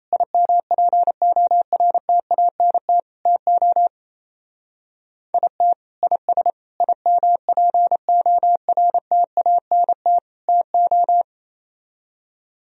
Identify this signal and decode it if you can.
Morse code, heard as a keyed tone, about 32 wpm, 700 Hz. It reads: IMPORTANT TO IT IS IMPORTANT TO
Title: Speed-Racing - Sets of 4 Words 28wpm